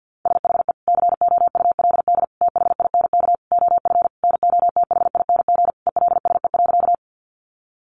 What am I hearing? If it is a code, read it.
Morse code, heard as a keyed tone, about 50 wpm, 700 Hz. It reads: SHE COULD THINK OF NOTHING ELSE.
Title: Sentences from Top 800 Words 50wpm